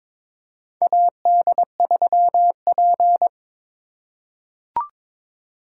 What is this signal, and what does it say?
Morse code, heard as a keyed tone, about 22 wpm, 700 Hz. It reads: AD3P
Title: USA Call Signs - 4 Characters - 4x Character Spacing - 22wpm